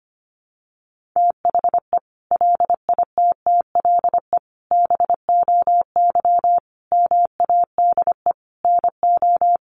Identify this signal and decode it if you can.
Morse code, heard as a keyed tone, about 25 wpm, 700 Hz. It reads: THE LITTLE BOY MADE NO
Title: Sentences from Top 1400 Words 25wpm